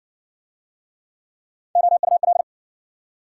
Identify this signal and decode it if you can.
Morse code, heard as a keyed tone, about 58 wpm, 700 Hz. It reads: OUL